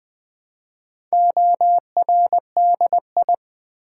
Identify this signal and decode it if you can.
Morse code, heard as a keyed tone, about 20 wpm, 700 Hz. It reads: ORDI
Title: Build A Word - 2 Letters At A Time - 8-Letter Words - 20wpm